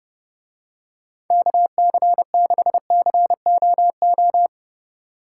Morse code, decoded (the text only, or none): KC6COO